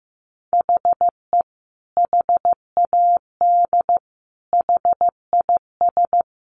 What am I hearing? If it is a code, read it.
Morse code, heard as a keyed tone, about 14 wpm, 700 Hz. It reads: HE HAD HIS
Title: Sentences from Top 900 Words 15wpm